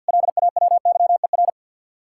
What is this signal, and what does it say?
Morse code, heard as a keyed tone, about 62 wpm, 700 Hz. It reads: LAWYER
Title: Speed-Racing Common QSO Phrases Encore - 50wpm